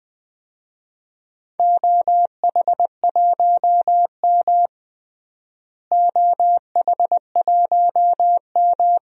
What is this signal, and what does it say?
Morse code, heard as a keyed tone, about 20 wpm, 700 Hz. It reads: OH1M OH1M
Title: DXCC-100 Call Signs - 4 Characters - Repeated 2x - 20wpm